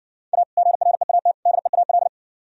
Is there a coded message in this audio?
Yes: Morse code from 0.3 to 2.1 s, heard as a keyed tone, about 59 wpm, 700 Hz. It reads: A GREAT DEAL